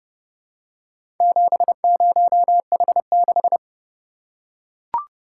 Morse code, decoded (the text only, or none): 70H6